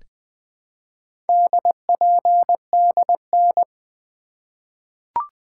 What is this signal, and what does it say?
Morse code, heard as a keyed tone, about 21 wpm, 700 Hz. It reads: DPDN
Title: Four Letter-Number Combinations - 20wpm